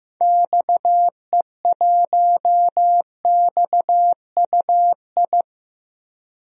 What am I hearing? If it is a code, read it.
Morse code, heard as a keyed tone, about 15 wpm, 700 Hz. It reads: XE1XUI
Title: DXCC-100 Call Signs - 6 Characters - 15wpm